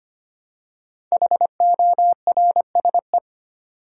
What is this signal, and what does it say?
Morse code, heard as a keyed tone, about 25 wpm, 700 Hz. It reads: HORSE
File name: 5 Letter Words - First 3 Letters Given - 25wpm